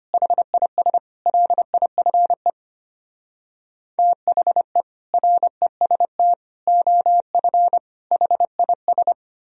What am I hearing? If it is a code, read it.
Morse code, heard as a keyed tone, about 32 wpm, 700 Hz. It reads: HIS LIFE THE REST OF HIS